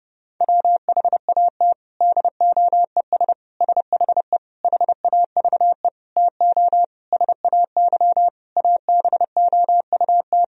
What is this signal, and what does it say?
Morse code, heard as a keyed tone, about 30 wpm, 700 Hz. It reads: WHAT DOES SHE HAVE TO SAY ABOUT
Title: Sentences from Top 1000 Words - Review Part 3 - 30wpm